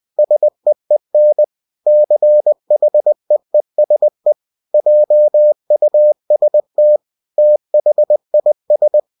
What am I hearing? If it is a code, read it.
Morse code, heard as a keyed tone, about 20 wpm, 600 Hz. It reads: SEEN CHEESE JUST THIS
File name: cw-books / winnie-the-pooh / winnie-20wpm-page-0037